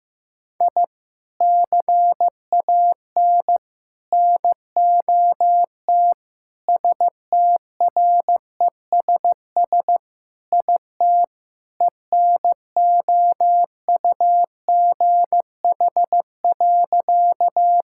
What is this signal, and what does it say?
Morse code, heard as a keyed tone, about 15 wpm, 700 Hz. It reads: I CAN NOT STRESS IT ENOUGH.
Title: Sentences from Top 1000 Words - Review Part 6 - 15wpm